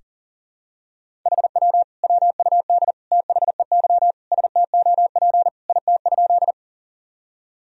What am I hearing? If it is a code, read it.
Morse code, heard as a keyed tone, about 40 wpm, 700 Hz. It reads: HW WUD THEY STOP IT?
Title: Sentences from Top 300 Words and Abbreviations 40wpm